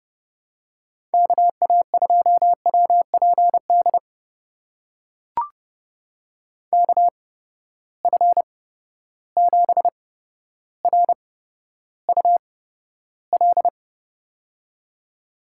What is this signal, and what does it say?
Morse code, heard as a keyed tone, about 30 wpm, spaced as for 13 wpm, 700 Hz. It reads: KA2WPD KF7RUL
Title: USA Call Signs - 6 Characters - 8x Character Spacing - 30wpm